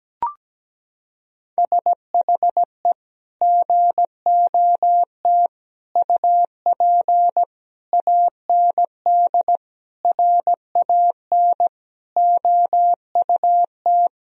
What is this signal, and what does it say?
Morse code, heard as a keyed tone, about 17 wpm, 700 Hz. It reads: SHE GOT UP AND RAN OUT
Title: Sentences from Top 1000 Words - Review Part 3 - 17wpm